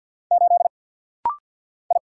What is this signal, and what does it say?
Morse code, heard as a keyed tone, about 48 wpm, 700 Hz. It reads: 8 I
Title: Single Letter-Number - Rapid-Fire 50wpm